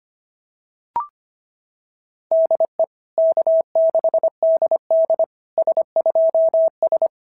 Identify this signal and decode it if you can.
Morse code, heard as a keyed tone, about 25 wpm, 650 Hz. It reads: DE K6DD S2S
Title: SOTA QSO Practice 25wpm